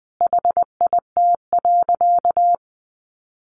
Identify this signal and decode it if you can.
Morse code, heard as a keyed tone, about 20 wpm, 700 Hz. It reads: HIT.
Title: Sentences from Top 1200 Words 20wpm